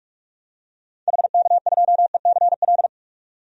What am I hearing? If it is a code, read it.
Morse code, heard as a keyed tone, about 46 wpm, 700 Hz. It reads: HK2ECL
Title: DXCC-100 Call Signs - 6 Characters - 45wpm